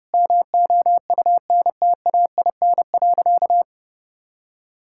Morse code, heard as a keyed tone, about 30 wpm, 700 Hz. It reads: MOUNTAIN.
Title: Sentences from Top 1000 Words - Review Part 6 - 30wpm